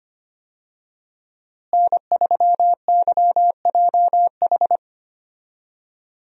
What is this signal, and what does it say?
Morse code, heard as a keyed tone, about 25 wpm, 700 Hz. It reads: N3YJH
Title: USA Call Signs - 5 Characters - Repeated 3x - 25wpm